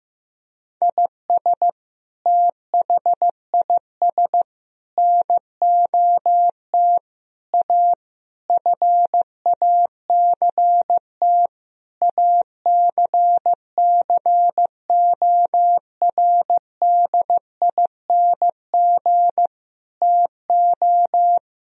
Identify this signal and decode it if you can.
Morse code, heard as a keyed tone, about 15 wpm, 700 Hz. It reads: IS THIS NOT A FACT ACCORDING TO
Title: Sentences from Top 1000 Words - Review Part 4 - 15wpm